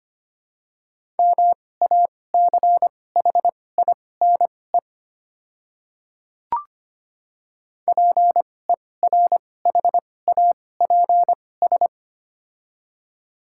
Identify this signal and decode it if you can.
Morse code, heard as a keyed tone, about 25 wpm, spaced as for 18 wpm, 700 Hz. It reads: MACHINE PERHAPS
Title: 2x Character Spacing - Top 500 Words - 25wpm